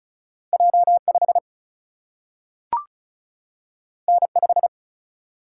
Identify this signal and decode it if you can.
Morse code, heard as a keyed tone, about 36 wpm, 700 Hz. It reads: J5 N5
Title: Single-Letter Single-Number 35wpm